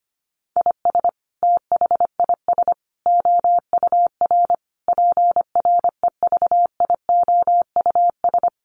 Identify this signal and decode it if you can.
Morse code, heard as a keyed tone, about 25 wpm, 700 Hz. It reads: IS THIS OUR PREVIOUS
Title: Sentences from Top 1500 Words 25wpm